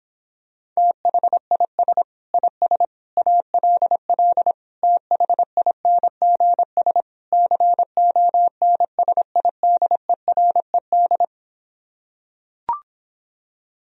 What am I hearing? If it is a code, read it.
Morse code, heard as a keyed tone, about 26 wpm, 700 Hz. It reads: THIS IS ALL THINGS CONSIDERED